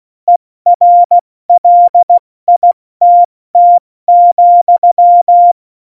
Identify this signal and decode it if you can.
Morse code, heard as a keyed tone, about 16 wpm, 700 Hz. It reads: ERLITT,
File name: Morsekurs / Witze / Witz 20. Oktober 2020, 70 F80